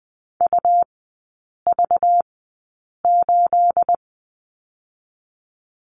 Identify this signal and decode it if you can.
Morse code, heard as a keyed tone, about 20 wpm, 700 Hz. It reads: U V 8
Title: 212 Lesson 34e - Number 8 - Three Characters 20wpm